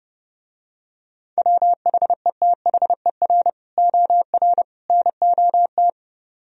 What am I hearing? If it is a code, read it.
Morse code, heard as a keyed tone, about 29 wpm, 700 Hz. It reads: WHETHER OR NOT